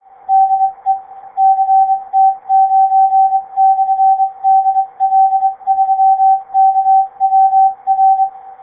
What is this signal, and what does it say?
Morse code, heard as a keyed tone, about 25 wpm, spaced as for 22 wpm, 770 Hz. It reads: DE CT9/DL3KWR